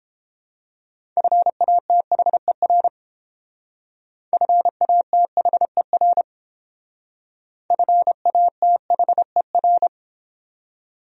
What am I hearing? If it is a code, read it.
Morse code, heard as a keyed tone, about 29 wpm, 700 Hz. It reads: FATHER FATHER FATHER